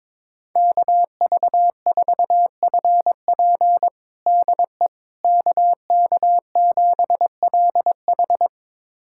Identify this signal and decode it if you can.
Morse code, heard as a keyed tone, about 22 wpm, 700 Hz. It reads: KV4FP DE KK7LH
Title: USA Callsign DE Callsign - 5 Characters - 22wpm